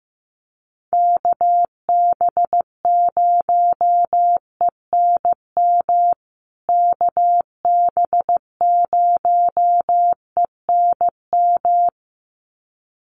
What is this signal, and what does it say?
Morse code, heard as a keyed tone, about 15 wpm, 700 Hz. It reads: KB0ENM KB0ENM
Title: USA Call Signs Repeated 2x 15wpm